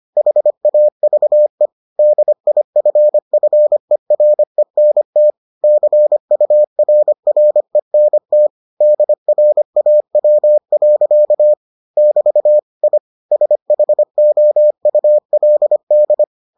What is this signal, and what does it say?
Morse code, heard as a keyed tone, about 25 wpm, 600 Hz. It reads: HAVE DIFFERENT CURRENT DRAW. <BT> I SHOULD